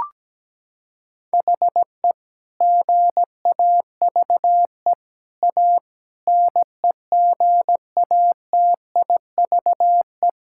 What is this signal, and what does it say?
Morse code, heard as a keyed tone, about 17 wpm, 700 Hz. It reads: HE GAVE A NEGATIVE
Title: Sentences from Top 1000 Words - Review Part 6 - 17wpm